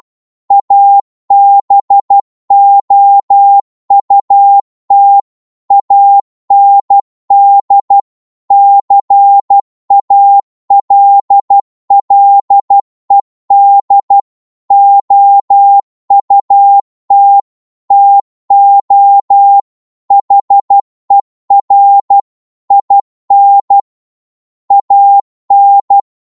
Morse code, heard as a keyed tone, about 12 wpm, 800 Hz. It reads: ABOUT AND CALLED OUT TO HER IN AN